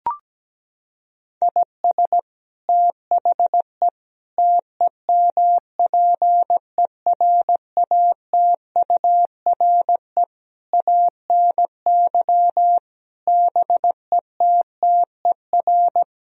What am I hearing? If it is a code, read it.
Morse code, heard as a keyed tone, about 17 wpm, 700 Hz. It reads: IS THE TEMPERATURE ANY BETTER
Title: Sentences from Top 1000 Words - Review Part 4 - 17wpm